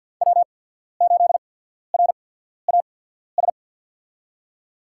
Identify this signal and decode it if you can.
Morse code, heard as a keyed tone, about 52 wpm, spaced as for 14 wpm, 700 Hz. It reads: W8RAS
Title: USA Call Signs - 5 Characters - 8x Character Spacing - 50wpm